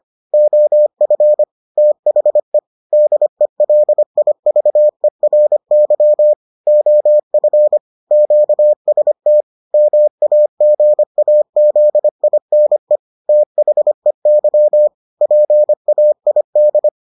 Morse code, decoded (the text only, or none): OF THE DELIVERY OF QST MAGAZINE THEY PAID